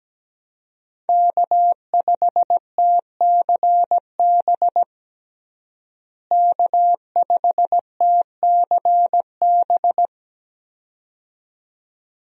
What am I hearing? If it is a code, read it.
Morse code, heard as a keyed tone, about 17 wpm, 700 Hz. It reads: K5TCB K5TCB
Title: USA Call Signs - 5 Characters - Repeated 2x - 17wpm